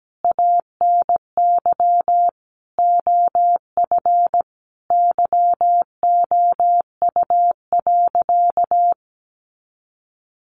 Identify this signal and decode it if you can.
Morse code, heard as a keyed tone, about 17 wpm, 700 Hz. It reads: ANY OF YOU.
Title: Sentences from Top 400 Words 17wpm